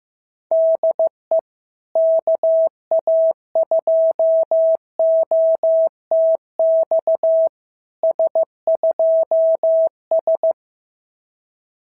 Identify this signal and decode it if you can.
Morse code, heard as a keyed tone, about 15 wpm, 650 Hz. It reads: DE KA2OTX S2S
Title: SOTA QSO Practice 15wpm